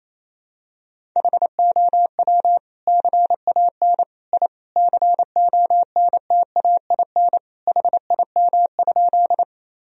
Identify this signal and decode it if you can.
Morse code, heard as a keyed tone, about 28 wpm, 700 Hz. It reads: HOW CAN I CONTAIN HIM?
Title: Sentences from Top 1000 Words - Review Part 2 - 28wpm